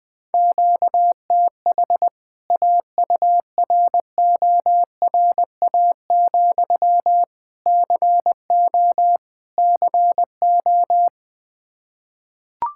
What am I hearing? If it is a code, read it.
Morse code, heard as a keyed tone, about 20 wpm, 700 Hz. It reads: QTH AURORA, CO CO
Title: K1USN SST QSO Practice 20wpm